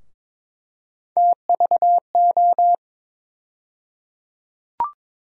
Morse code, heard as a keyed tone, about 22 wpm, 700 Hz. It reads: TVO
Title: USA Call Sign Postfixes - 3 Letters 22wpm